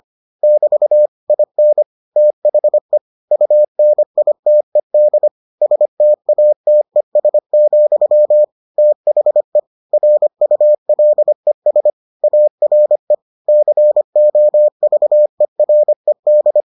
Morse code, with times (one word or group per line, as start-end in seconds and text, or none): 0.4-1.1 <BT>
1.3-1.8 IN
2.2-3.0 THE
3.3-5.3 UNITED
5.6-8.4 STATES,
8.8-9.6 THE
9.9-11.9 RULES
12.2-13.1 ARE
13.5-16.6 COVERED